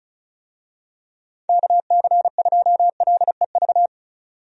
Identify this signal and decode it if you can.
Morse code, heard as a keyed tone, about 36 wpm, 700 Hz. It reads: KC2LEV
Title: USA Call Signs - 6 Characters - 35wpm